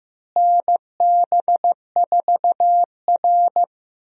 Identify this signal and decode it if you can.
Morse code, heard as a keyed tone, about 15 wpm, 700 Hz. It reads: NB4R
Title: USA Call Signs - 4 Characters - Repeated 2x - 15wpm